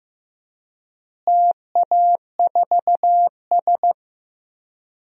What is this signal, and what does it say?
Morse code, heard as a keyed tone, about 15 wpm, 700 Hz. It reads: TA4S
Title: DXCC-100 Call Signs - 4 Characters - Repeated 3x - 15wpm